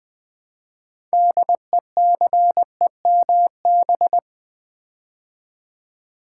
Morse code, heard as a keyed tone, about 20 wpm, 700 Hz. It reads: DECEMB
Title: Build A Word - 3 Letters At A Time - 8-Letter Words - 20wpm